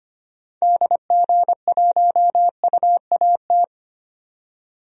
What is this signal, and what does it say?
Morse code, heard as a keyed tone, about 25 wpm, 700 Hz. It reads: DG1UAT
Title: DXCC-100 Call Signs - 6 Characters - 25wpm